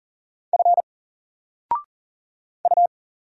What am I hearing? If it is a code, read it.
Morse code, heard as a keyed tone, about 40 wpm, 700 Hz. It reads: F U
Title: Single Character (Letter:Number:Punctuation:Prosign) 40wpm